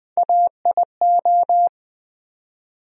Morse code, heard as a keyed tone, about 21 wpm, 700 Hz. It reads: AIO
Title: USA Call Sign Postfixes - 3 Letters 20wpm